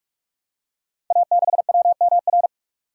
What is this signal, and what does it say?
Morse code, heard as a keyed tone, about 44 wpm, 700 Hz. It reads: A6WMR
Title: Code Groups - 5 Letter:Numbers - 45wpm